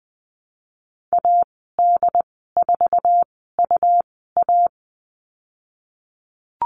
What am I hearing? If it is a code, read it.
Morse code, heard as a keyed tone, about 20 wpm, spaced as for 15 wpm, 700 Hz. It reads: AD4UA
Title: 179 Lesson 29g - Slash - Call Signs 20wpm